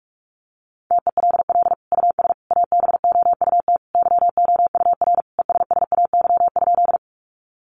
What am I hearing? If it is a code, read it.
Morse code, heard as a keyed tone, about 44 wpm, 700 Hz. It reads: TELL US ABOUT YOUR ESSAY?
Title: Sentences from Top 1500 Words 45wpm